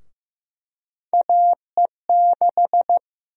Morse code, heard as a keyed tone, about 15 wpm, 700 Hz. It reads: AE6
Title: USA Call Sign Prefixes - Repeated 2x - 15wpm